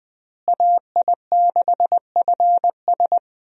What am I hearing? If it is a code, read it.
Morse code, heard as a keyed tone, about 20 wpm, 700 Hz. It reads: AI6FS
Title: USA Call Signs - 5 Characters - Repeated 3x - 20wpm